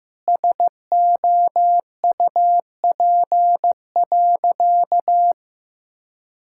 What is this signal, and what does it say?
Morse code, heard as a keyed tone, about 16 wpm, 700 Hz. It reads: SOUP.
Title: Sentences from Top 1000 Words - Review Part 6 - 15wpm